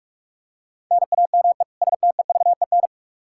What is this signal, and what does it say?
Morse code, heard as a keyed tone, about 45 wpm, 700 Hz. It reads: NAME STEVEN
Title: Practice QSO - Exchange 1 - Beginning - 45wpm